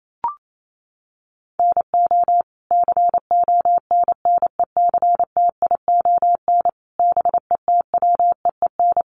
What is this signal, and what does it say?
Morse code, heard as a keyed tone, about 28 wpm, 700 Hz. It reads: NO CONNECTION BETWEEN
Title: Sentences from Top 1300 Words 28wpm